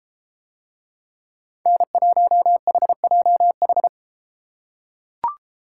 Morse code, heard as a keyed tone, about 31 wpm, 700 Hz. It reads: N1HJH